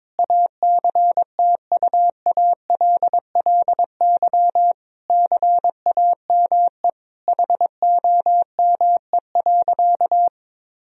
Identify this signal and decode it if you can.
Morse code, heard as a keyed tone, about 21 wpm, 700 Hz. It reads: ACTUALLY CAME HOME.
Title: Sentences from Top 1000 Words - Review Part 4 - 22wpm